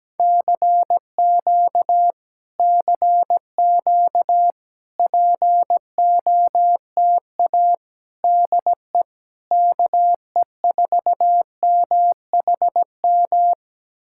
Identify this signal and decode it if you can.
Morse code, heard as a keyed tone, about 17 wpm, 700 Hz. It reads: CQ CQ POTA DE KE4MHM
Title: Speed-Racing Common QSO Phrases - 17wpm